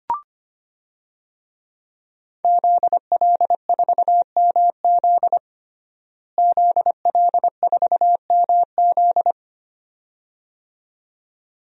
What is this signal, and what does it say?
Morse code, heard as a keyed tone, about 25 wpm, 700 Hz. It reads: ZL4MZ ZL4MZ
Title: DXCC-100 Call Signs - 5 Characters - Repeated 2x - 25wpm